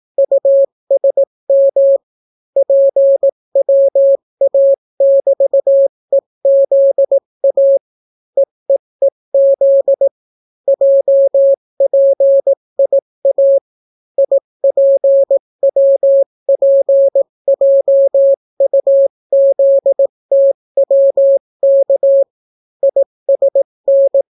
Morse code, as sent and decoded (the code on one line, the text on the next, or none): ..- ... -- / .--. .-- .- -...- . --.. .- / . . --.. / .--- .--. .. .- / .. .--. .-- .--. .--- ..- --.. - .-- -.- / .. ... -.
USM PWA<BT>EZA EEEZ JPIA IPWPJUZTWK ISN